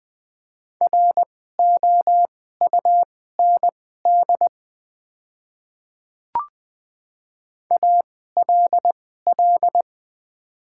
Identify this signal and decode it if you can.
Morse code, heard as a keyed tone, about 20 wpm, spaced as for 14 wpm, 690 Hz. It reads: ROUND ALL
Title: 2x Character Spacing - Top 500 Words - 20wpm